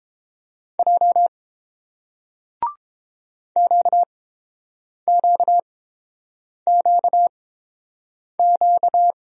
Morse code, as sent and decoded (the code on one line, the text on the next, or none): .--- / --.- --.- --.- --.-
J QQQQ